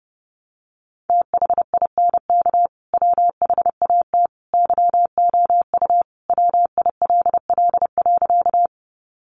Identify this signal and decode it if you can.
Morse code, heard as a keyed tone, about 30 wpm, 700 Hz. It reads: THINK WHAT YOU WILL.
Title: Sentences from Top 200 Words 30wpm